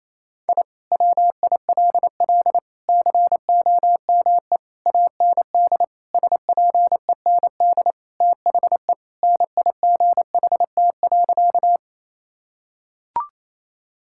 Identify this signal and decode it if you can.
Morse code, heard as a keyed tone, about 28 wpm, 700 Hz. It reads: I WILL COME AND SPEND THE NIGHT.
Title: Sentences from Top 1000 Words - Review Part 5 - 28wpm